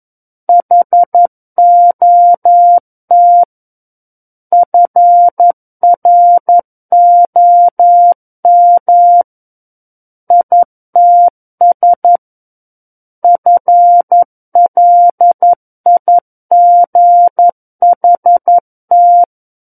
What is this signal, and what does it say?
Morse code, heard as a keyed tone, about 11 wpm, 700 Hz. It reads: HOT FROM ITS FLIGHT
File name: Book One / 02-Chapter 2 - 11 WPM